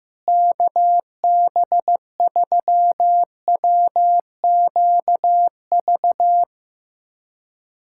Morse code, heard as a keyed tone, about 15 wpm, 700 Hz. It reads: KB3WQV
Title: USA Callsign DE Callsign 15wpm